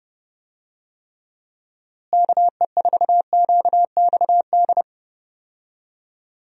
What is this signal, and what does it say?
Morse code, heard as a keyed tone, about 31 wpm, 700 Hz. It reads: KE4QXD
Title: USA Call Signs - 6 Characters - Repeated 3x - 30wpm